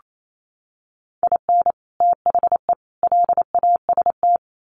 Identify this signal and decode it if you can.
Morse code, heard as a keyed tone, about 28 wpm, 700 Hz. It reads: IN THE LAST
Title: Sets of 3 Words 28wpm